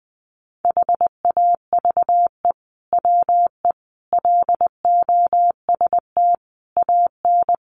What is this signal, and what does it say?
Morse code, heard as a keyed tone, about 20 wpm, 700 Hz. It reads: HAVE WE LOST AN
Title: Sentences from Top 1400 Words 20wpm